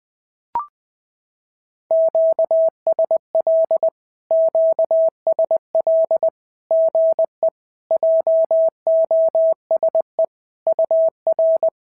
Morse code, heard as a keyed tone, about 20 wpm, 650 Hz. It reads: QSL QSL GE JOSE UR
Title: SOTA QSO Practice 20wpm